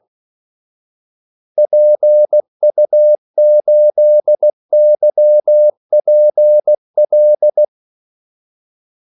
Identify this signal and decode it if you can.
Morse code, heard as a keyed tone, about 16 wpm, 600 Hz. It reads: PU8YPL
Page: Kaldesignaler 16 wpm